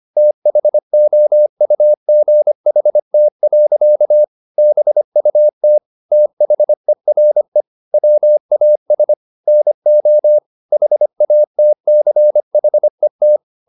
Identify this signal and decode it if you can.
Morse code, heard as a keyed tone, about 25 wpm, 600 Hz. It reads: THOUGHT. BUT THERE WAS NO HATCHET